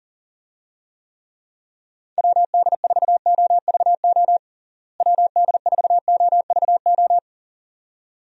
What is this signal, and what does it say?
Morse code, heard as a keyed tone, about 37 wpm, 700 Hz. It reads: WD4OVO WD4OVO
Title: USA Call Signs - 6 Characters - Repeated 2x - 40wpm